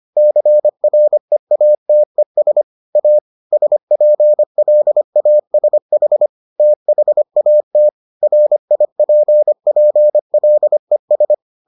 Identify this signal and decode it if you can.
Morse code, heard as a keyed tone, about 25 wpm, 600 Hz. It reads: CREATES A SPLASH THAT RIPPLES